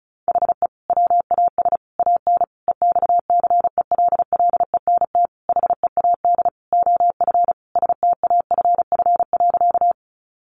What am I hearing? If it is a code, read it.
Morse code, heard as a keyed tone, about 34 wpm, 700 Hz. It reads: HE WAS AN EXCELLENT HEAD OF STAFF.
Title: Sentences from Top 1200 Words 35wpm